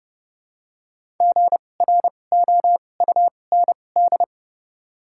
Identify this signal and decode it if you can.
Morse code, heard as a keyed tone, about 30 wpm, spaced as for 22 wpm, 700 Hz. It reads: GROUND
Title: 2x Character Spacing - Top 500 Words - 30wpm